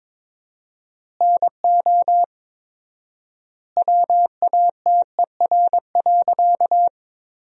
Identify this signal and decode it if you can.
Morse code, heard as a keyed tone, about 21 wpm, 700 Hz. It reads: NO WATER.
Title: 4x Word-Spacing - Sentences from Top 100 Words - 22wpm